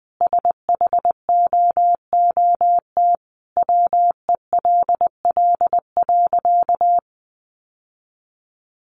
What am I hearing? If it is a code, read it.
Morse code, heard as a keyed tone, about 20 wpm, 700 Hz. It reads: SHOOT WELL.
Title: Sentences from Top 1200 Words 20wpm